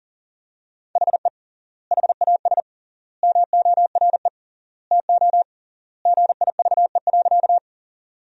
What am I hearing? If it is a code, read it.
Morse code, heard as a keyed tone, about 40 wpm, 700 Hz. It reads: HE HAS MORE TO GIVE.
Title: 3x Word-Spacing - Sentences from Top 200 Words - 40wpm